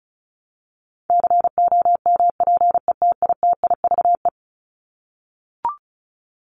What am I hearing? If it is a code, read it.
Morse code, heard as a keyed tone, about 34 wpm, 700 Hz. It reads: COMPETITIVE
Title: Top 1001-1250 Words 35wpm